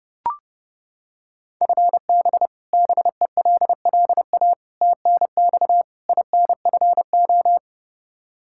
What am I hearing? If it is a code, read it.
Morse code, heard as a keyed tone, about 31 wpm, 700 Hz. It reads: FB BELLA TNX INFO
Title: Practice QSO - Exchange 3 - More Info - 30wpm